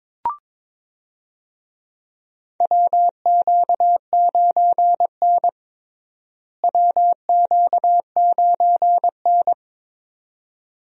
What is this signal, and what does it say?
Morse code, heard as a keyed tone, about 22 wpm, 700 Hz. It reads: WQ9N WQ9N
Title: USA Call Signs - 4 Characters - Repeated 2x - 22wpm